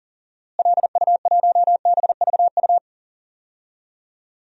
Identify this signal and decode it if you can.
Morse code, heard as a keyed tone, about 40 wpm, 700 Hz. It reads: LU1BVU
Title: DXCC-100 Call Signs - 6 Characters - Repeated 3x - 40wpm